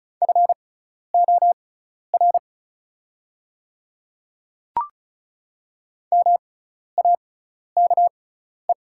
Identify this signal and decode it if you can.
Morse code, heard as a keyed tone, about 36 wpm, spaced as for 12 wpm, 700 Hz. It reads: FOR MAKE
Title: 6x Character Spacing - Top 100 Words - 35wpm